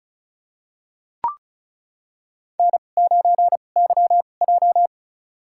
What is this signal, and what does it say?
Morse code, heard as a keyed tone, about 34 wpm, spaced as for 25 wpm, 700 Hz. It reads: N9YJ
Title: USA Call Signs - 4 Characters - 2x Character Spacing - 35wpm